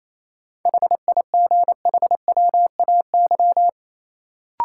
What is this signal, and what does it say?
Morse code, heard as a keyed tone, about 28 wpm, 700 Hz. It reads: HIGHWAY
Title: Sentences from Top 1400 Words - New Word Drill Alt - 28wpm